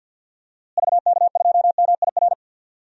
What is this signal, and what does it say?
Morse code, heard as a keyed tone, about 53 wpm, 700 Hz. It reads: VK2MIR
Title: DXCC-100 Call Signs - 6 Characters - Repeated 2x - 50wpm